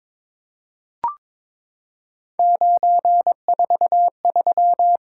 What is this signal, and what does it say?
Morse code, heard as a keyed tone, about 22 wpm, 700 Hz. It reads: 943
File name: Three Letter-Number Combinations - 22wpm